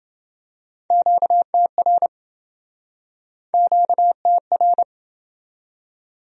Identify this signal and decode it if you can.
Morse code, heard as a keyed tone, about 29 wpm, 700 Hz. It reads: QTR QTR